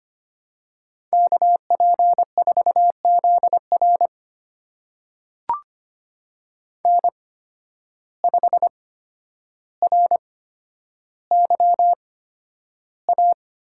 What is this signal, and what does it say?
Morse code, heard as a keyed tone, about 25 wpm, spaced as for 11 wpm, 700 Hz. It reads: KP4ZR N5RYA
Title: USA Call Signs - 8x Character Spacing - 25wpm